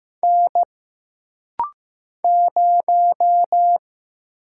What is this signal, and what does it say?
Morse code, heard as a keyed tone, about 15 wpm, 700 Hz. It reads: N 0
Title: Single Letter-Number Alt - Rapid-Fire 15wpm